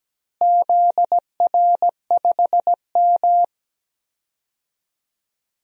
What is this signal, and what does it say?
Morse code, heard as a keyed tone, about 17 wpm, 700 Hz. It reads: ZR5M
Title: DXCC-100 Call Signs - 4 Characters - Repeated 3x - 17wpm